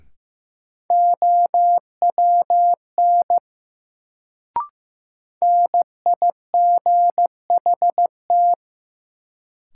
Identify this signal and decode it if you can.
Morse code, heard as a keyed tone, about 15 wpm, 700 Hz. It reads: OWN NIGHT
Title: Sentences from Top 300 Words - New Word Drill - 15wpm